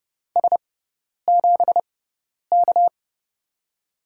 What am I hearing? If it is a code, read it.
Morse code, heard as a keyed tone, about 30 wpm, spaced as for 10 wpm, 700 Hz. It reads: S7K
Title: USA Call Signs - 4 Characters - 6x Character Spacing - 30wpm